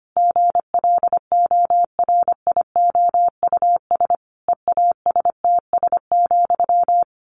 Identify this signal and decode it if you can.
Morse code, heard as a keyed tone, about 25 wpm, 700 Hz. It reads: GLORIOUS EASTS,
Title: A Sound Of Thunder 25wpm